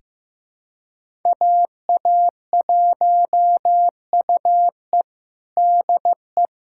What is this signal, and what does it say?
Morse code, heard as a keyed tone, about 15 wpm, 700 Hz. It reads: AA1UE DE
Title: USA Callsign DE Callsign - 5 Characters - 15wpm